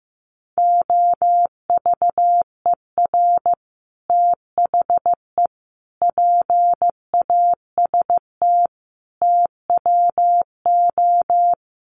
Morse code, heard as a keyed tone, about 15 wpm, 700 Hz. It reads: OVER THE PAST TWO
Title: Sets of 5 Words Encore 15wpm